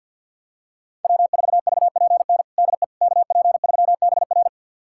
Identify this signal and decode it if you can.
Morse code, heard as a keyed tone, about 51 wpm, 700 Hz. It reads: W4VPN DE KP3BR
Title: USA Callsign DE Callsign - 5 Characters - 50wpm